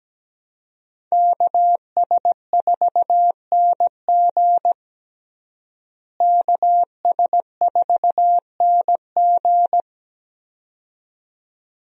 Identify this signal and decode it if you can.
Morse code, heard as a keyed tone, about 17 wpm, 700 Hz. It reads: KS4NG KS4NG
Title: USA Call Signs - 5 Characters - Repeated 2x - 17wpm